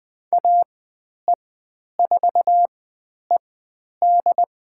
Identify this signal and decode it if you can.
Morse code, heard as a keyed tone, about 20 wpm, spaced as for 10 wpm, 700 Hz. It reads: AE4ED
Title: 109 Lesson 18g - Letter m - Call Signs 20wpm